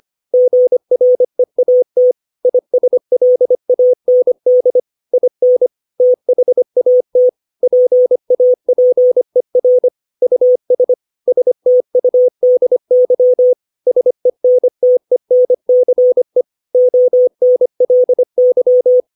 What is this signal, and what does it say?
Morse code, heard as a keyed tone, about 25 wpm, 500 Hz. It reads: GREAT ISLAND IN THAT PAPER US STUDY SENTENCE ONLY
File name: 632 words / 632 Words 25WPM